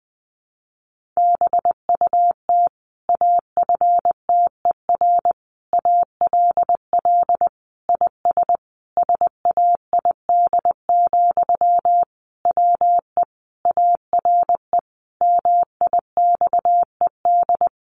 Morse code, heard as a keyed tone, about 20 wpm, 700 Hz. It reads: BUT AFTER ALL IS SAID, WE ARE MIXED
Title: Sentences from Top 1500 Words 20wpm